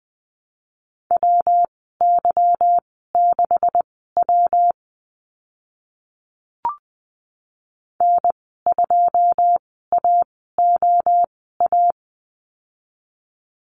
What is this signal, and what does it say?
Morse code, heard as a keyed tone, about 20 wpm, spaced as for 14 wpm, 700 Hz. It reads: WY6W N2AOA
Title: 161 Lesson 26g - Letter g - Call Signs 20wpm